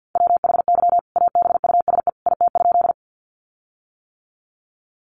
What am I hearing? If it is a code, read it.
Morse code, heard as a keyed tone, about 50 wpm, 700 Hz. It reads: WHY ABUSE IT?
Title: Sentences from Top 1300 Words 50wpm